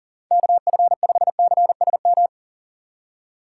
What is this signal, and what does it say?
Morse code, heard as a keyed tone, about 53 wpm, spaced as for 47 wpm, 700 Hz. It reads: KF5CSM
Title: Speed Racing - USA 2x3 Call Signs 40wpm